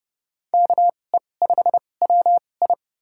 Morse code, heard as a keyed tone, about 31 wpm, spaced as for 22 wpm, 700 Hz. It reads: KE5WI
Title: USA Call Signs - 5 Characters - 2x Character Spacing - 30wpm